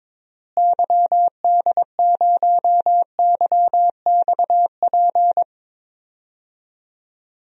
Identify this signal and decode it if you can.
Morse code, heard as a keyed tone, about 22 wpm, 700 Hz. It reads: YD0YXP
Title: DXCC-100 Call Signs - 6 Characters - Repeated 2x - 22wpm